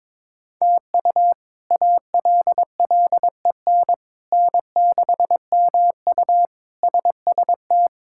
Morse code, heard as a keyed tone, about 22 wpm, 700 Hz. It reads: TU ALLEN N6MU SST
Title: K1USN SST QSO Practice - 22wpm